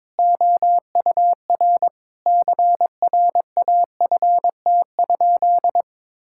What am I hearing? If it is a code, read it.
Morse code, heard as a keyed tone, about 22 wpm, 700 Hz. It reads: OUR CRAFT?
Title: Sentences from Top 1000 Words - Review Part 6 - 22wpm